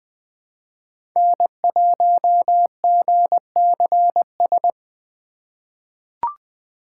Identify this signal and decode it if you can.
Morse code, heard as a keyed tone, about 20 wpm, 700 Hz. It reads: N1GCS
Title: USA Call Signs - 6x Character Spacing - 20wpm